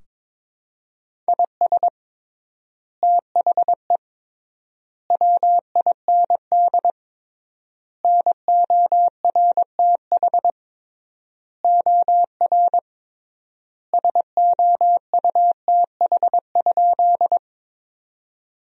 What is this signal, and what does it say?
Morse code, heard as a keyed tone, about 22 wpm, 700 Hz. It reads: IS THE WIND NORTH OR SOUTH?
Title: 3x Word-Spacing - Sentences from Top 400 Words - 22wpm